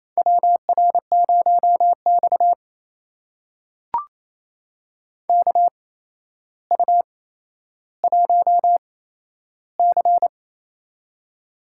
Morse code, heard as a keyed tone, about 29 wpm, spaced as for 12 wpm, 700 Hz. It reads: WR0X KU1C
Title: USA Call Signs - 4 Characters - 8x Character Spacing - 28wpm